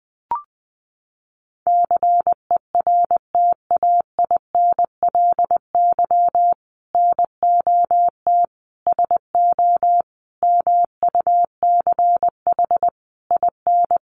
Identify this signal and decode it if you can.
Morse code, heard as a keyed tone, about 20 wpm, 700 Hz. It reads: CERTAINLY NOT SO MUCH IN
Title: Sentences from Top 1500 Words 20wpm